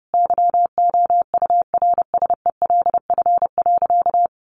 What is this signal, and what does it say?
Morse code, heard as a keyed tone, about 30 wpm, 700 Hz. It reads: YOURSELF.
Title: Sentences from Top 1500 Words 30wpm